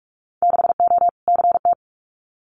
Morse code, heard as a keyed tone, about 46 wpm, 700 Hz. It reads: 6O CT
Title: Winter Field Day Exchanges Encore 45wpm